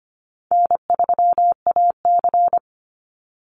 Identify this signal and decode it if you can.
Morse code, heard as a keyed tone, about 25 wpm, 700 Hz. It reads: N3AC
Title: Call Signs Repeated 2x - 25wpm